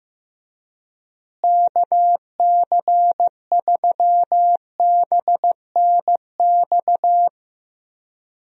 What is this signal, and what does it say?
Morse code, heard as a keyed tone, about 15 wpm, 700 Hz. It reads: KC3BNX
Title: USA Call Signs - 6 Characters - Repeated 2x - 15wpm